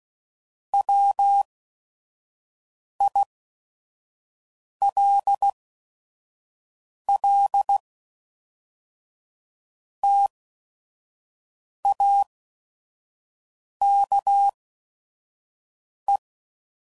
Morse code, heard as a keyed tone, about 16 wpm, spaced as for 5 wpm, 800 Hz. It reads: WILL TAKE
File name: CW Code Groups @5WPM / Technical Subjects / OR Gate